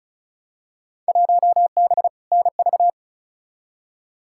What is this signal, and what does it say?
Morse code, heard as a keyed tone, about 36 wpm, 700 Hz. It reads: 1B NV
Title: Field Day Exchanges - 35wpm